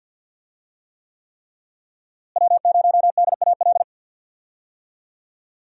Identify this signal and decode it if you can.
Morse code, heard as a keyed tone, about 48 wpm, 700 Hz. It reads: W0DAL
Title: USA Call Signs - 5 Characters - Repeated 3x - 50wpm